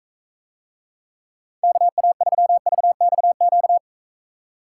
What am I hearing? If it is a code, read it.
Morse code, heard as a keyed tone, about 44 wpm, 700 Hz. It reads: KA3VXQ